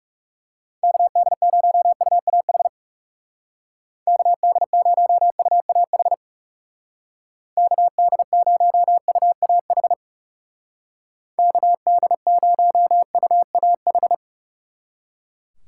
Morse code, heard as a keyed tone, about 36 wpm, 700 Hz. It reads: KD0UAH KD0UAH KD0UAH KD0UAH
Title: Speed Racing - USA Call Signs 30wpm